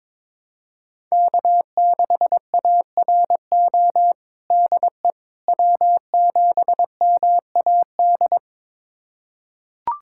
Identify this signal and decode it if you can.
Morse code, heard as a keyed tone, about 22 wpm, 700 Hz. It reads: K6ARO DE W7MAD
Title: USA Callsign DE Callsign - 5 Characters - 22wpm